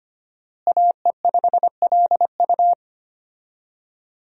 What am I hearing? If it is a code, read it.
Morse code, heard as a keyed tone, about 25 wpm, 700 Hz. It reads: AE5LU
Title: USA Call Signs - 5 Characters - Repeated 3x - 25wpm